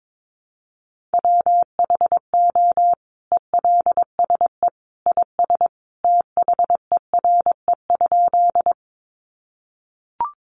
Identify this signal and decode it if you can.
Morse code, heard as a keyed tone, about 22 wpm, 700 Hz. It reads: WHO ELSE IS THERE?
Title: Sentences from Top 800 Words 22wpm